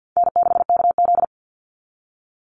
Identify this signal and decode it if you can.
Morse code, heard as a keyed tone, about 49 wpm, 700 Hz. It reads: N6KZ
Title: NA SOTA Chaser Callsigns 50wpm